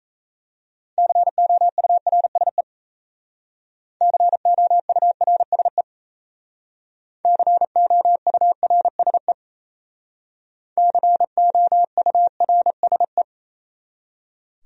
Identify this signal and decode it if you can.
Morse code, heard as a keyed tone, about 35 wpm, 700 Hz. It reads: COURSE COURSE COURSE COURSE
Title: Speed-Racing Top 500 Words 28wpm